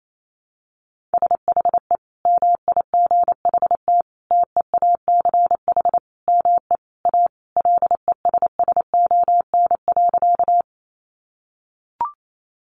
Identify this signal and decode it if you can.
Morse code, heard as a keyed tone, about 28 wpm, 700 Hz. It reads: SHE MIGHT TEACH ME A LESSON.
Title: Sentences from Top 1200 Words 28wpm